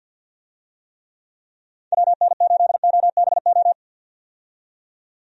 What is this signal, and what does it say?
Morse code, heard as a keyed tone, about 50 wpm, 700 Hz. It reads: WN8OBO
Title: USA Call Signs - 6 Characters - 50wpm